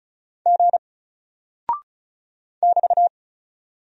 Morse code, heard as a keyed tone, about 36 wpm, 700 Hz. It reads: G <BT>
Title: Single Character (Letter:Number:Punctuation:Prosign) Alt 35wpm